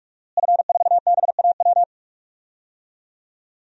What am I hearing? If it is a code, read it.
Morse code, heard as a keyed tone, about 46 wpm, 700 Hz. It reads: F4BAW
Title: DXCC-100 Call Signs - 5 Characters - Repeated 3x - 45wpm